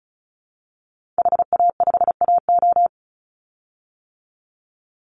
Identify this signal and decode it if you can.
Morse code, heard as a keyed tone, about 34 wpm, 700 Hz. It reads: HA5AO
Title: Call Signs - 35wpm